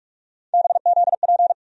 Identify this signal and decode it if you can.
Morse code, heard as a keyed tone, about 46 wpm, 700 Hz. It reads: BZP
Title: USA Call Sign Postfixes - 3 Letters 45wpm